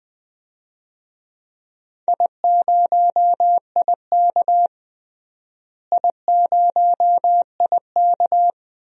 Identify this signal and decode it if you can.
Morse code, heard as a keyed tone, about 20 wpm, 700 Hz. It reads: I0IK I0IK
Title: DXCC-100 Call Signs - 4 Characters - Repeated 2x - 20wpm